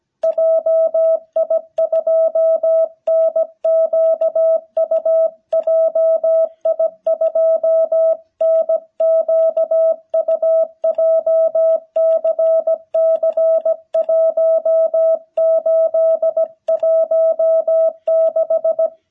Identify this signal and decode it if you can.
Morse code, heard as a keyed tone, about 17 wpm, 660 Hz. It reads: JI2NQU JI2NQU JCC1816
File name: jh8cbh / 2021cw